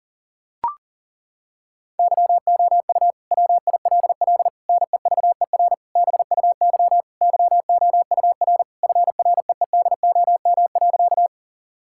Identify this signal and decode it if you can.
Morse code, heard as a keyed tone, about 38 wpm, 700 Hz. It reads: YOU WILL NEVER BUY YOUR FREEDOM.
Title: Sentences from Top 1000 Words - Review Part 5 - 40wpm